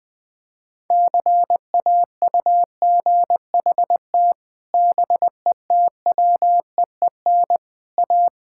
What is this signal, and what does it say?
Morse code, heard as a keyed tone, about 20 wpm, 700 Hz. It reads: CAUGHT BETWEEN A
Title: Sets of 8 Words - 20wpm